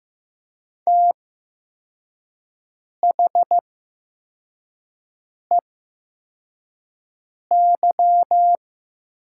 Morse code, heard as a keyed tone, about 15 wpm, spaced as for 4 wpm, 700 Hz. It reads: THEY
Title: 8x Character Spacing - Top 200 Words - 15wpm